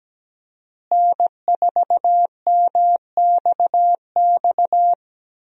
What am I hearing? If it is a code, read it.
Morse code, heard as a keyed tone, about 17 wpm, 700 Hz. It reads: N4MXX
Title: USA Call Signs - 5 Characters - Repeated 3x - 17wpm